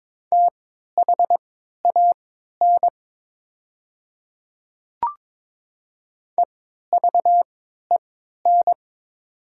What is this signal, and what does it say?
Morse code, heard as a keyed tone, about 22 wpm, spaced as for 13 wpm, 700 Hz. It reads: THAN EVEN
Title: 3x Character Spacing - Top 200 Words - 22wpm